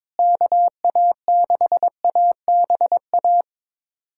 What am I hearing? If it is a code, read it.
Morse code, heard as a keyed tone, about 22 wpm, 700 Hz. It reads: KA6ABA
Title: USA Callsign DE Callsign - 6 Characters - 22wpm